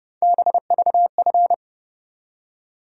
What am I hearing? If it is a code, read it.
Morse code, heard as a keyed tone, about 30 wpm, 700 Hz. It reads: BVF
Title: USA Call Sign Postfixes - 3 Letters 30wpm